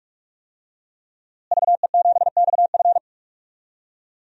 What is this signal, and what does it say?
Morse code, heard as a keyed tone, about 41 wpm, 700 Hz. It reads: VE7XF
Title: DXCC-100 Call Signs - 5 Characters - Repeated 2x - 45wpm